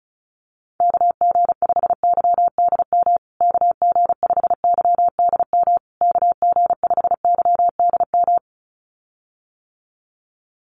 Morse code, heard as a keyed tone, about 34 wpm, 700 Hz. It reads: KG5YDM KG5YDM KG5YDM
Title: USA Call Signs Repeated 3x 35wpm